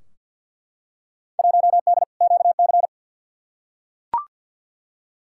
Morse code, heard as a keyed tone, about 51 wpm, 700 Hz. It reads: 1D QC
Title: Field Day Exchanges - 50wpm